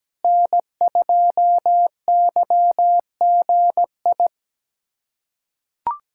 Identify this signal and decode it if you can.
Morse code, heard as a keyed tone, about 17 wpm, 700 Hz. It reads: N2YGI
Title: USA Callsign DE Callsign 17wpm